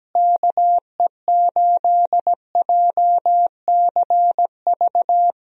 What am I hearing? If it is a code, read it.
Morse code, heard as a keyed tone, about 17 wpm, 700 Hz. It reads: KE8JCV
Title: USA Call Signs - 6 Characters - Repeated 2x - 17wpm